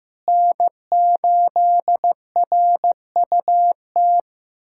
Morse code, heard as a keyed tone, about 16 wpm, 700 Hz. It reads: N8RUT
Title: USA Callsign DE Callsign - 5 Characters - 15wpm